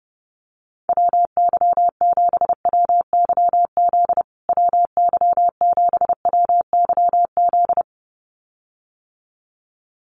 Morse code, heard as a keyed tone, about 30 wpm, 700 Hz. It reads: WY7WYZ WY7WYZ
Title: USA Call Signs Repeated 2x 30wpm